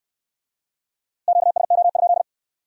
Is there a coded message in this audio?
Yes: Morse code from 1.3 to 2.2 s, heard as a keyed tone, about 60 wpm, 700 Hz. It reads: QSO?